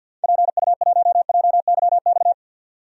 Morse code, heard as a keyed tone, about 52 wpm, 700 Hz. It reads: PU1JYX
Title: DXCC-100 Call Signs - 6 Characters - 50wpm